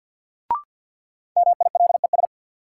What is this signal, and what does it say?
Morse code, heard as a keyed tone, about 48 wpm, 700 Hz. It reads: MILES
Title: Sentences from Top 1000 Words - Review Part 3 - New Word Drill - 50wpm